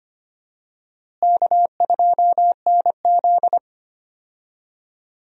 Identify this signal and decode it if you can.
Morse code, heard as a keyed tone, about 25 wpm, 700 Hz. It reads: K2NZ
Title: USA Call Signs - 4 Characters - Repeated 3x - 25wpm